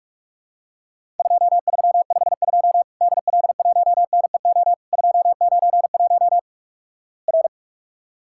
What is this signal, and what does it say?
Morse code, heard as a keyed tone, about 48 wpm, 700 Hz. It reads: 2352 DL1NEO 291 R
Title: Worked All Europe DX Contest - QTC Exchange - 45wpm